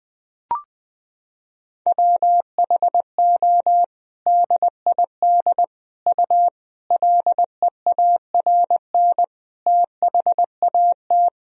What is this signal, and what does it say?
Morse code, heard as a keyed tone, about 20 wpm, 700 Hz. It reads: WHO DID U LEARN THAT
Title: Sentences from Top 300 Words and Abbreviations 20wpm